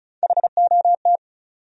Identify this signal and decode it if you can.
Morse code, heard as a keyed tone, about 34 wpm, 700 Hz. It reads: HOT
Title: Top 100 Words Repeated 35wpm